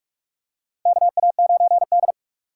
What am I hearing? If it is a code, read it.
Morse code, heard as a keyed tone, about 46 wpm, 700 Hz. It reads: KA9D
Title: USA Call Signs - 4 Characters - 4x Character Spacing - 45wpm